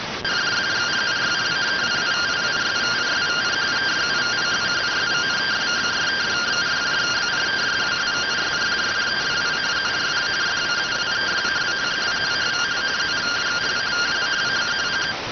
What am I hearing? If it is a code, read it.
Two-tone teleprinter signal, about 170 Hz shift.